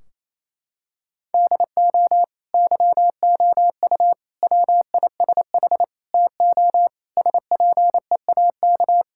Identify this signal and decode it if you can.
Morse code, heard as a keyed tone, about 28 wpm, 700 Hz. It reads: DO YOU WISH TO SPEAK
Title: Sentences from Top 1600 Words 28wpm